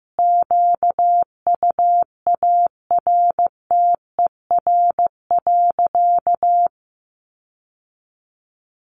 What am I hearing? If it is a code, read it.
Morse code, heard as a keyed tone, about 15 wpm, 700 Hz. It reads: QUARTER.
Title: Sentences from Top 1500 Words 15wpm